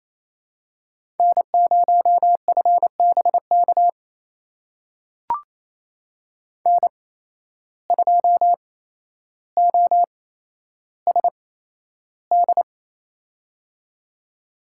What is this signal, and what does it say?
Morse code, heard as a keyed tone, about 29 wpm, spaced as for 12 wpm, 700 Hz. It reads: N0FBK N2OSD
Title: USA Call Signs - 5 Characters - 8x Character Spacing - 28wpm